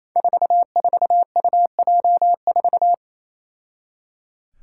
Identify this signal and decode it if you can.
Morse code, heard as a keyed tone, about 28 wpm, 700 Hz. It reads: 44UJ4
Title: Five Challenging Letter-Number Combinations 28wpm